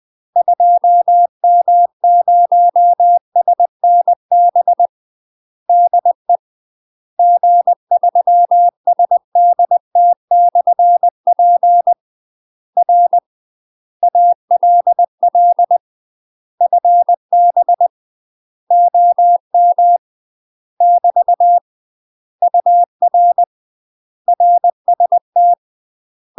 Morse code, heard as a keyed tone, about 20 wpm, 700 Hz. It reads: 2M0SNB DE G3SDT/P R ALL FB OM <BT> UR RST
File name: UK MASTER QSOs TRAINING / 20WPM / UK MASTER QSOs 0017